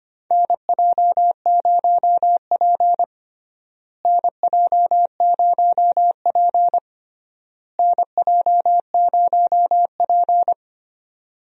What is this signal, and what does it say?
Morse code, heard as a keyed tone, about 25 wpm, 700 Hz. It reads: NJ0P NJ0P NJ0P
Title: USA Call Signs - 4 Characters - Repeated 3x - 25wpm